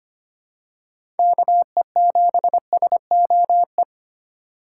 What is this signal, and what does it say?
Morse code, heard as a keyed tone, about 25 wpm, 700 Hz. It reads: KE7SOE
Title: USA Call Signs - 2x Character Spacing - 25wpm